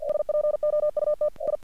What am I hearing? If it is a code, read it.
Morse code, heard as a keyed tone, about 45 wpm, 600 Hz. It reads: SPOUT I
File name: archived / CRSnet